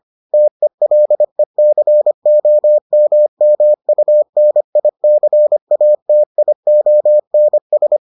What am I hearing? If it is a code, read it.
Morse code, heard as a keyed tone, about 25 wpm, 600 Hz. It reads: TELECOMMUNICATIONS